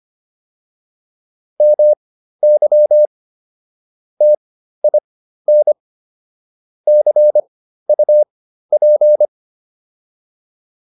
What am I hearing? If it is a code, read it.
Morse code, heard as a keyed tone, about 25 wpm, spaced as for 13 wpm, 600 Hz. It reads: MY TIN CUP